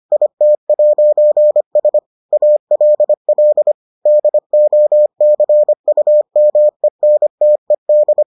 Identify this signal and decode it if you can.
Morse code, heard as a keyed tone, about 25 wpm, 600 Hz. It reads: IT'S ALL DOCUMENTED